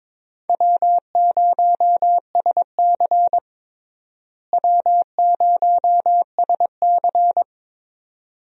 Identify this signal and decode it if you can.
Morse code, heard as a keyed tone, about 22 wpm, 700 Hz. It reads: W0SC W0SC
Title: USA Call Signs - 4 Characters - Repeated 3x - 22wpm